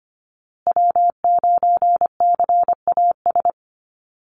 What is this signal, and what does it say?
Morse code, heard as a keyed tone, about 25 wpm, 700 Hz. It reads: W9CAS
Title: USA Call Signs Repeated 2x 25wpm